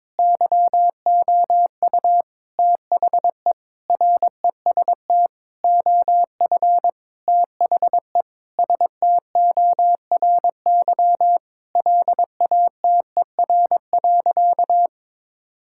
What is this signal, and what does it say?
Morse code, heard as a keyed tone, about 22 wpm, 700 Hz. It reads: YOU THE REST OF THE STORY LATER.
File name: practice / 22wpm / Sentences with 4-Word Phrases - 22wpm